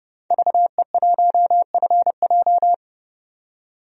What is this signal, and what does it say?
Morse code, heard as a keyed tone, about 29 wpm, 700 Hz. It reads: VE1FJ
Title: DXCC-100 Call Signs - 5 Characters - 30wpm